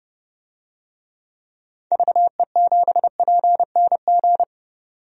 Morse code, heard as a keyed tone, about 30 wpm, 700 Hz. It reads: VE7PNG
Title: DXCC-100 Call Signs - 6 Characters - 30wpm